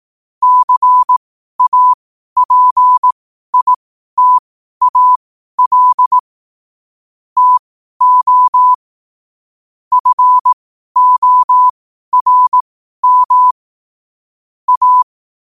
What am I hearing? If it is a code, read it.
Morse code, heard as a keyed tone, about 18 wpm, spaced as for 12 wpm, 1000 Hz. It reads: CAPITAL TO FORM A
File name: qotdfiles / 12wpm